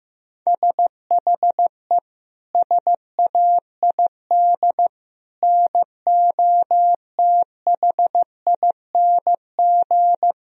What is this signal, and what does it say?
Morse code, heard as a keyed tone, about 16 wpm, 700 Hz. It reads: SHE SAID NOTHING
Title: Sentences from Top 1000 Words - Review Part 2 - 15wpm